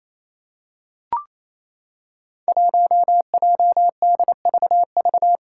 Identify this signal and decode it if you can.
Morse code, heard as a keyed tone, about 27 wpm, 700 Hz. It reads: 1JDVV
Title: Five Challenging Letter-Number Combinations 28wpm